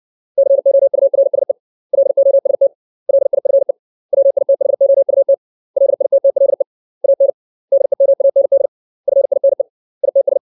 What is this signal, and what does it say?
Morse code, heard as a keyed tone, about 60 wpm, 550 Hz. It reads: COURSE LOST LIFE WITHOUT LITTLE AN DEMAND FINE ITS
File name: 632 words / 632 Words 60WPM